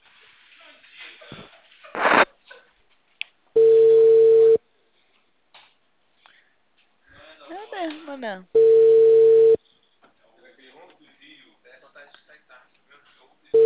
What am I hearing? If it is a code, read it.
Ringback tone (China, 450 Hz), with speech.